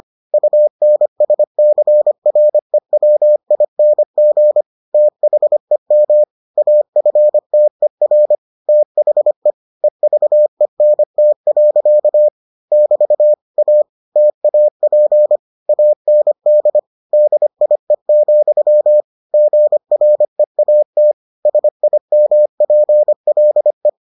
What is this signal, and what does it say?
Morse code, heard as a keyed tone, about 25 wpm, 600 Hz. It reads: UNSCREWING THEM AFTER THE EVENT. <BT> A TAP AND DIE, GREAT SIMPLE